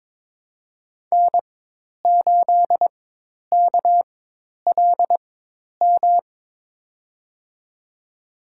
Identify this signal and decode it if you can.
Morse code, heard as a keyed tone, about 22 wpm, spaced as for 10 wpm, 700 Hz. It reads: N8KLM
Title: USA Call Signs - 5 Characters - 4x Character Spacing - 22wpm